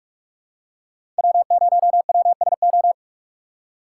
Morse code, heard as a keyed tone, about 44 wpm, 700 Hz. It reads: W0WSO
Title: USA Call Signs - 5 Characters - 6x Character Spacing - 45wpm